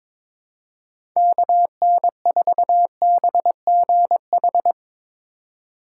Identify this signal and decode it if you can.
Morse code, heard as a keyed tone, about 22 wpm, 700 Hz. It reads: KN4BGH
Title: USA Call Signs - 6 Characters - 4x Character Spacing - 22wpm